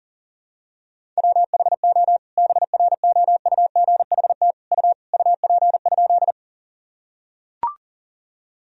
Morse code, heard as a keyed tone, about 40 wpm, 700 Hz. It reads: WHO BROUGHT U UP?
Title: Sentences from Top 500 Words and Abbreviations 40wpm